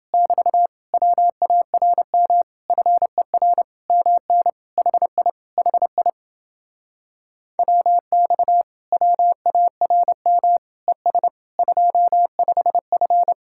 Morse code, heard as a keyed tone, about 30 wpm, 700 Hz. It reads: <BT> WARM FER MN HI HI WX WARM ES 25F